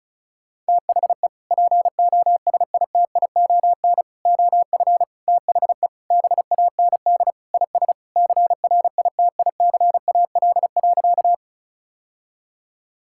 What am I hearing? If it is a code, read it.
Morse code, heard as a keyed tone, about 36 wpm, 700 Hz. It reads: THE POSITION OF THE BAND IS CRITICAL.
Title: Sentences from Top 1600 Words 35wpm